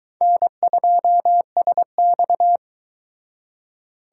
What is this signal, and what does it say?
Morse code, heard as a keyed tone, about 22 wpm, 700 Hz. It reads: N2SX